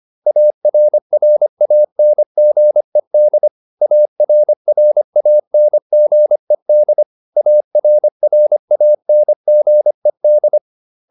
Morse code, heard as a keyed tone, about 25 wpm, 600 Hz. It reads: ARRANGED ARRANGED ARRANGED